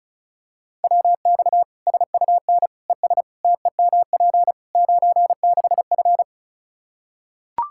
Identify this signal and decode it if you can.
Morse code, heard as a keyed tone, about 34 wpm, 700 Hz. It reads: WX SUN ES TEMP 96F
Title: Practice QSO - Exchange 2 - Info - 35wpm